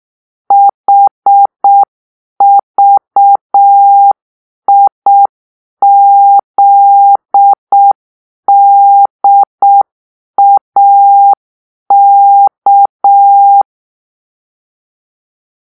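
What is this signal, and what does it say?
Morse code, heard as a keyed tone, about 6 wpm, 800 Hz. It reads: HVIZDAK
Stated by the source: Morse-code cipher made by a player team for the Kachna puzzle game.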